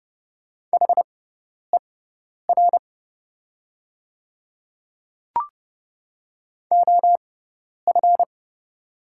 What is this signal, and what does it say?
Morse code, heard as a keyed tone, about 30 wpm, spaced as for 10 wpm, 700 Hz. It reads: HER OF
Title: 6x Character Spacing - Top 100 Words - 30wpm